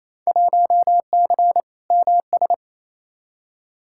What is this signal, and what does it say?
Morse code, heard as a keyed tone, about 28 wpm, 700 Hz. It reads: 1C MS
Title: Field Day Exchanges Encore - 28wpm